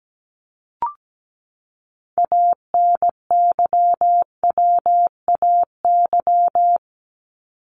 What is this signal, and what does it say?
Morse code, heard as a keyed tone, about 17 wpm, 700 Hz. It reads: ANYWAY
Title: Top 1501-1750 Words 17wpm